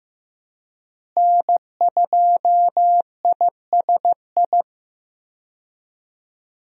Morse code, heard as a keyed tone, about 15 wpm, 700 Hz. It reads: N2ISI
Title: USA Call Signs - 5 Characters - Repeated 3x - 15wpm